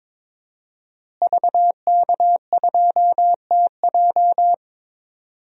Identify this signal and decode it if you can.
Morse code, heard as a keyed tone, about 22 wpm, 700 Hz. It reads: VK2TJ
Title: DXCC-100 Call Signs - 5 Characters - Repeated 2x - 22wpm